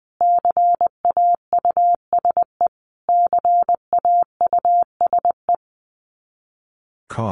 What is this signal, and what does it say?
Morse code, heard as a keyed tone, about 20 wpm, 700 Hz. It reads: CAUSE CAUSE
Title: Top 200 Words Repeated 20wpm